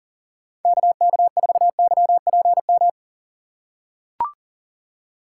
Morse code, heard as a keyed tone, about 40 wpm, 700 Hz. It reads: KK4YPM
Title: USA Call Signs - 6 Characters - 6x Character Spacing - 40wpm